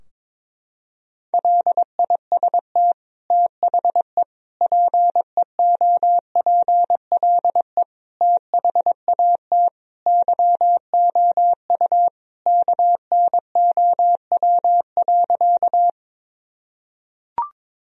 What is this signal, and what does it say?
Morse code, heard as a keyed tone, about 22 wpm, 700 Hz. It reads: LIST THE PEOPLE THAT YOU KNOW.
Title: Sentences from Top 1000 Words - Review Part 2 - 22wpm